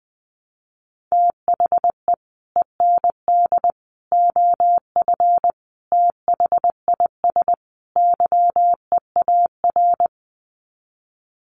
Morse code, heard as a keyed tone, about 20 wpm, 700 Hz. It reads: THE END OF THIS YEAR
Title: Sets of 5 Words Encore 20wpm